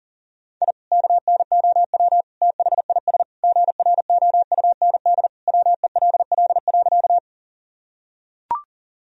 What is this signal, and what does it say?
Morse code, heard as a keyed tone, about 37 wpm, 700 Hz. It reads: I KNOW THIS GROUND WELL.
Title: Sentences from Top 1000 Words - Review Part 2 - 40wpm